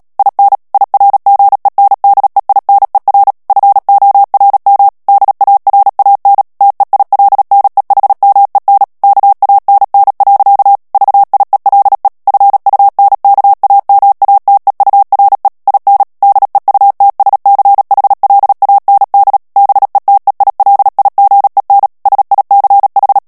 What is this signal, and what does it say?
Morse code, heard as a keyed tone, about 37 wpm, 800 Hz. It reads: IN IRGENDEINER FORM DARAN TEILNEHMEN KANN. VIELE FUNKAMATEURE IN DEUTSCHLAND BETEILIGEN SICH
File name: morse mp3 / ALLEMAND texte clair 2 vitesse 35 mots minute